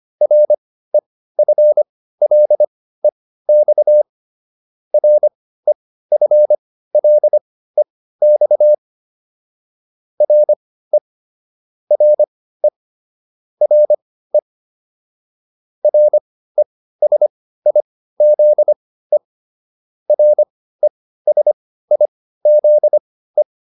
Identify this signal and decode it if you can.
Morse code, heard as a keyed tone, about 25 wpm, spaced as for 15 wpm, 600 Hz. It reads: REFLEX REFLEX RE RE RE RESIZE RESIZE